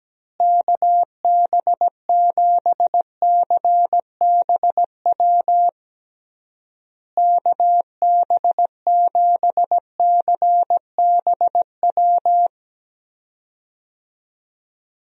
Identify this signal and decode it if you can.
Morse code, heard as a keyed tone, about 17 wpm, 700 Hz. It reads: KB7CBW KB7CBW
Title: USA Call Signs - 6 Characters - Repeated 3x - 17wpm